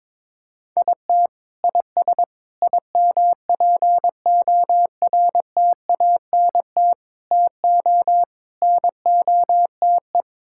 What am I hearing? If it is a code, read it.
Morse code, heard as a keyed tone, about 22 wpm, 700 Hz. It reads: IT IS IMPORTANT TO NOTE
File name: practice / 22wpm / Build A Phrase - 5-Word Phrases - 22wpm